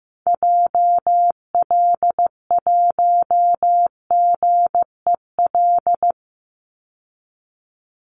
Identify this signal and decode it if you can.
Morse code, heard as a keyed tone, about 15 wpm, 700 Hz. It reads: JL1GEL
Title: Call Signs Repeated 3x - 15wpm